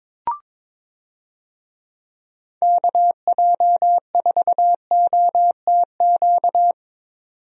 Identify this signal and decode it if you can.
Morse code, heard as a keyed tone, about 22 wpm, 700 Hz. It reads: KJ4OTQ
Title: USA Call Signs - 6 Characters - 22wpm